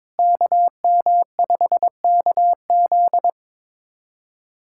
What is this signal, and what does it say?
Morse code, heard as a keyed tone, about 22 wpm, 700 Hz. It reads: KM5KZ
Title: USA Callsign DE Callsign - 5 Characters - 22wpm